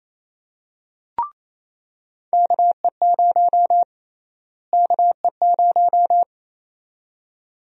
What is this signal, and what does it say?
Morse code, heard as a keyed tone, about 28 wpm, 700 Hz. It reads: KE0 KE0
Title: USA Call Sign Prefixes - Repeated 2x - 28wpm